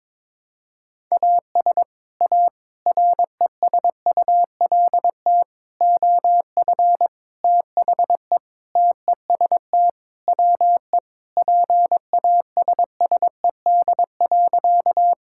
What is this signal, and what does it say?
Morse code, heard as a keyed tone, about 22 wpm, 700 Hz. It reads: AS A RESULT OF THE TEST WE PASSED.
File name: practice / 22wpm / Sentences with 4-Word Phrases - 22wpm